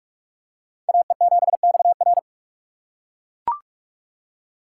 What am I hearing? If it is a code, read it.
Morse code, heard as a keyed tone, about 41 wpm, 700 Hz. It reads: AE7XR